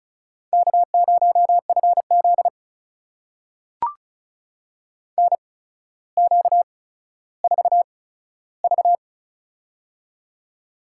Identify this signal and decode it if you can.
Morse code, heard as a keyed tone, about 36 wpm, spaced as for 15 wpm, 700 Hz. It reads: K0FZ NQ4V
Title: USA Call Signs - 4 Characters - 8x Character Spacing - 35wpm